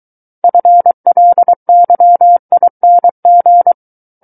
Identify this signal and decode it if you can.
Morse code, heard as a keyed tone, about 23 wpm, 700 Hz. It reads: FLYING
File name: Book One / 14-Chapter 14 - 23 WPM